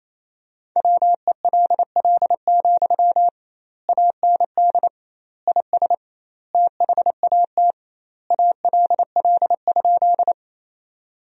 Morse code, heard as a keyed tone, about 27 wpm, 700 Hz. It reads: WELL, AND IS THAT ALL?
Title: 2x Word-Spacing - Sentences from Top 200 Words - 28wpm